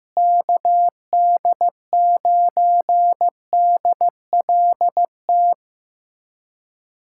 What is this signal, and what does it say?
Morse code, heard as a keyed tone, about 15 wpm, 700 Hz. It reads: KD9DLT
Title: USA Callsign DE Callsign 15wpm